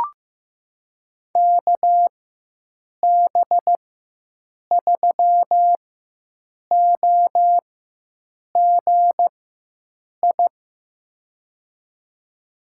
Morse code, heard as a keyed tone, about 15 wpm, spaced as for 7 wpm, 700 Hz. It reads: KB3OGI
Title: USA Call Signs - 6 Characters - 4x Character Spacing - 15wpm